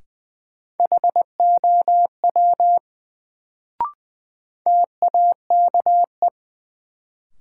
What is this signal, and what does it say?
Morse code, heard as a keyed tone, about 20 wpm, 700 Hz. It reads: HOW TAKE
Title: Sets of 2 Words - New Word Drill - 20wpm